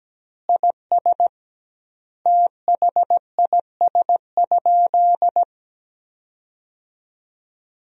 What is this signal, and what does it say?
Morse code, heard as a keyed tone, about 17 wpm, 700 Hz. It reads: IS THIS?
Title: 2x Word-Spacing - Sentences from Top 100 Words - 17wpm